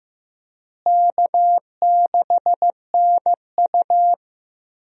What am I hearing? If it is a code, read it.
Morse code, heard as a keyed tone, about 15 wpm, 700 Hz. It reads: K6NU
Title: USA Call Signs - 4 Characters - Repeated 2x - 15wpm